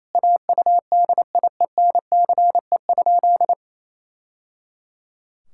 Morse code, heard as a keyed tone, about 28 wpm, 700 Hz. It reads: AUDIENCE?
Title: Sentences from Top 1000 Words - Review Part 5 - 28wpm